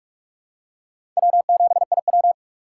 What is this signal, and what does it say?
Morse code, heard as a keyed tone, about 46 wpm, 700 Hz. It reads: W7IW
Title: USA Call Signs - 4 Characters - 3x Character Spacing - 45wpm